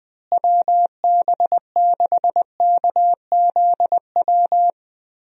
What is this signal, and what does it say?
Morse code, heard as a keyed tone, about 20 wpm, 700 Hz. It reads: WB6KZW
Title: USA Call Signs - 6 Characters - Repeated 3x - 20wpm